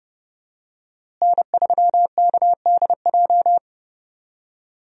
Morse code, heard as a keyed tone, about 32 wpm, 700 Hz. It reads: N3KDJ
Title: USA Call Signs - 5 Characters - Repeated 2x - 30wpm